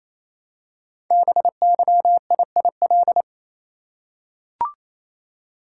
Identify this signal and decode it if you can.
Morse code, heard as a keyed tone, about 29 wpm, 700 Hz. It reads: BYIIL
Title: Code Groups - 5 Letters - 28wpm